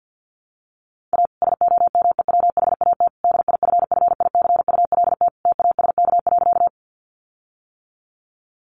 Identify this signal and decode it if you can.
Morse code, heard as a keyed tone, about 50 wpm, 700 Hz. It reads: A SOMEWHAT DIFFICULT TASK.
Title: Sentences from Top 1300 Words 50wpm